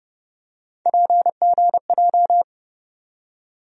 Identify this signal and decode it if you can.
Morse code, heard as a keyed tone, about 30 wpm, 700 Hz. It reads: PGJ
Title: USA Call Sign Postfixes - 3 Letters 30wpm